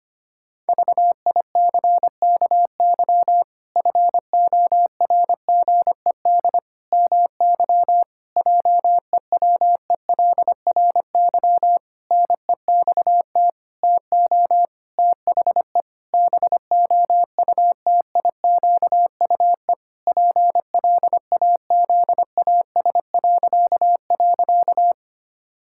Morse code, heard as a keyed tone, about 25 wpm, 700 Hz. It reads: VICKY FORGED MY JEWELRY NEXT TO THE BOUTIQUE PLAZAS..
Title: Pangram Challenge v2 - 25wpm